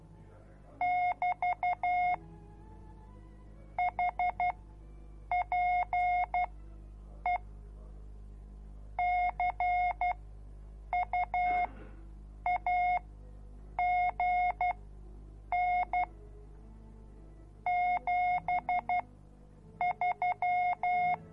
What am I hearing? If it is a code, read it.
Morse code, heard as a keyed tone, about 12 wpm, spaced as for 7 wpm, 720 Hz. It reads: <BT> HPE CUAGN 73